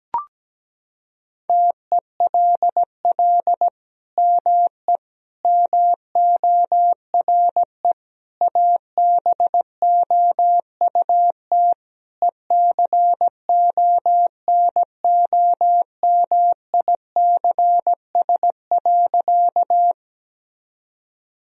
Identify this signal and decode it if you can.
Morse code, heard as a keyed tone, about 17 wpm, 700 Hz. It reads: TELL ME MORE ABOUT ECONOMICS.
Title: Sentences from Top 1000 Words - Review Part 3 - 17wpm